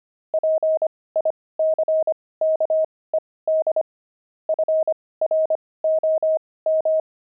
Morse code, heard as a keyed tone, about 25 wpm, spaced as for 18 wpm, 630 Hz. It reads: PICKED FROM
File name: pangram-fs-18-cs-25-sixty